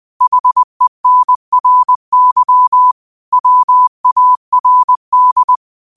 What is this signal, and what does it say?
Morse code, heard as a keyed tone, about 20 wpm, 1000 Hz. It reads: HENRY WARD
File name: qotdfiles / 20wpm